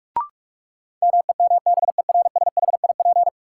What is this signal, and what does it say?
Morse code, heard as a keyed tone, about 45 wpm, 700 Hz. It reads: MEMBERSHIP
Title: Sentences from Top 1400 Words - New Word Drill Alt - 45wpm